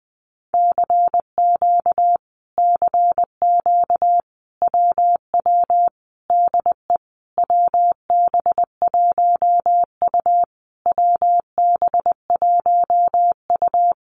Morse code, heard as a keyed tone, about 20 wpm, 700 Hz. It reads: CQ CQ WW DE WB1U WB1U
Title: Common QSO Phrases Encore 20wpm